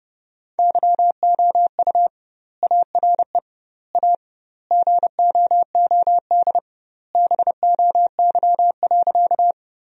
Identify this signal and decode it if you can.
Morse code, heard as a keyed tone, about 30 wpm, 700 Hz. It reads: YOU ARE A GOOD BOY.
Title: 2x Word-Spacing - Sentences from Top 200 Words - 30wpm